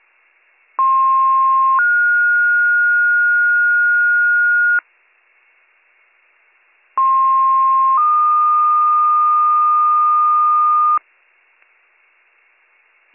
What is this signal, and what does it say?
Two-tone page, heard as a one-second tone, then a three-second tone, about 1060 then 1430 Hz, about 1060 then 1220 Hz. It is heard from 0.8 to 11.0 s.